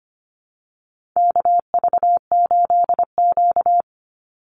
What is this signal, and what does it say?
Morse code, heard as a keyed tone, about 25 wpm, 700 Hz. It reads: KV8Q
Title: Call Signs - 25wpm